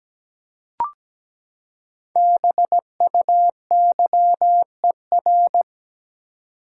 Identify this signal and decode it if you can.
Morse code, heard as a keyed tone, about 17 wpm, 700 Hz. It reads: BUYER
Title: Five Letter Words - 17wpm